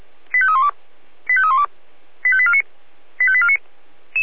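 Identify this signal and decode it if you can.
Five-tone selective call (ZVEI), heard as quick runs of five tones, tones of about 70 ms. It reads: 85312 85312 85849 85849
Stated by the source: ZVEI five-tone call received on SDR.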